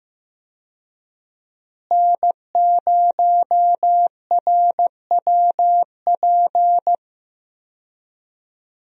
Morse code, heard as a keyed tone, about 15 wpm, 700 Hz. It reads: N0RWP
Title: USA Call Signs - 5 Characters - Repeated 2x - 15wpm